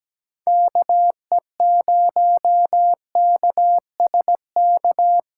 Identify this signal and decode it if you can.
Morse code, heard as a keyed tone, about 17 wpm, 700 Hz. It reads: KE0KSK
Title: USA Callsign DE Callsign 17wpm